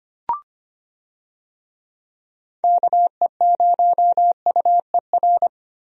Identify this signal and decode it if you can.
Morse code, heard as a keyed tone, about 25 wpm, 700 Hz. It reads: KE0UER
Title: USA Call Signs - 6 Characters - Repeated 3x - 25wpm